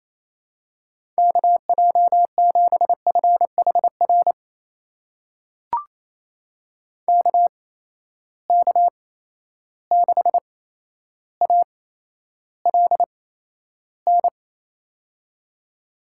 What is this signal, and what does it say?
Morse code, heard as a keyed tone, about 28 wpm, spaced as for 12 wpm, 700 Hz. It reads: KJ7FHR KK6ALN
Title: USA Call Signs - 6 Characters - 8x Character Spacing - 28wpm